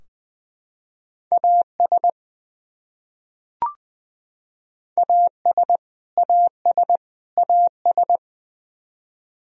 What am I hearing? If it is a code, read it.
Morse code, heard as a keyed tone, about 21 wpm, 700 Hz. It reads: AS AS AS AS
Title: Top 100 Words Repeated 20wpm